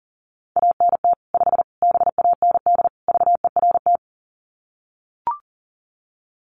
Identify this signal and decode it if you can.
Morse code, heard as a keyed tone, about 40 wpm, 700 Hz. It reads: ANT 5 BAND VERT
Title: Common QSO Phrases 40wpm